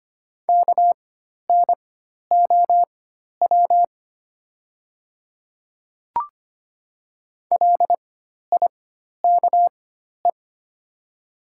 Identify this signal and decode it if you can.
Morse code, heard as a keyed tone, about 25 wpm, spaced as for 12 wpm, 700 Hz. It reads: KNOW LIKE
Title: 4x Character Spacing - Top 100 Words - 25wpm